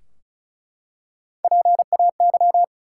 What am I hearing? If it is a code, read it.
Morse code, heard as a keyed tone, about 36 wpm, 700 Hz. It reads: PAY
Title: Three Letter Words - 35wpm